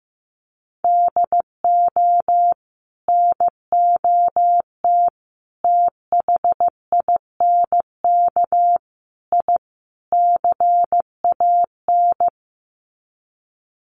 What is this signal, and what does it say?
Morse code, heard as a keyed tone, about 15 wpm, 700 Hz. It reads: DO NOT THINK I CAN
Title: Sets of 5 Words Encore2 15wpm